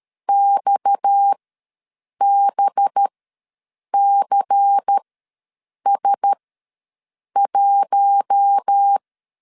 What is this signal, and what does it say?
Morse code, heard as a keyed tone, about 13 wpm, spaced as for 7 wpm, 800 Hz. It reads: XBCS1
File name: LEZIONI / 8 YJ45 / 8 Y ws 12wpm 3409